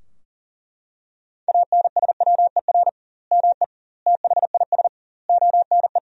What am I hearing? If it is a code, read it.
Morse code, heard as a keyed tone, about 40 wpm, 700 Hz. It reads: ANSWER ME THIS ONE
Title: 2x Word-Spacing - Sentences from Top 400 Words - 40wpm